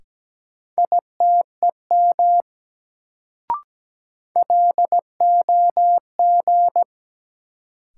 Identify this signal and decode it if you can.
Morse code, heard as a keyed tone, about 17 wpm, 700 Hz. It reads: ITEM LOG
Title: Sentences from Top 1400 Words - New Word Drill Alt - 17wpm